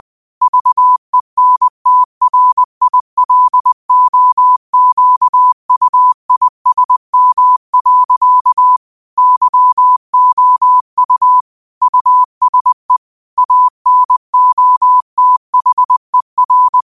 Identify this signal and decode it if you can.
Morse code, heard as a keyed tone, about 20 wpm, 1000 Hz. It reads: VENTRILOQUISM. YOU USE ANOTHER
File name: qotdfiles / 20wpm